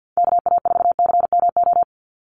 Morse code, heard as a keyed tone, about 50 wpm, 700 Hz. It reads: KA4CMO
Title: USA Call Signs 50wpm